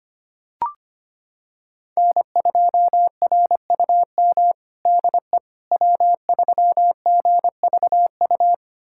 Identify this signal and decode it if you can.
Morse code, heard as a keyed tone, about 25 wpm, 700 Hz. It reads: N2RUM DE W3GVU
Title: USA Callsign DE Callsign - 5 Characters - 25wpm